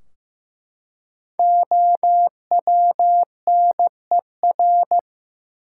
Morse code, heard as a keyed tone, about 15 wpm, 700 Hz. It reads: OWNER
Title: Five Letter Words - 15wpm